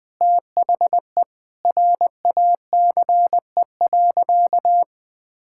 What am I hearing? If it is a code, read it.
Morse code, heard as a keyed tone, about 20 wpm, 700 Hz. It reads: THE RACE.
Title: Sentences with 8-Word Phrases - Spoken Beginning - 20wpm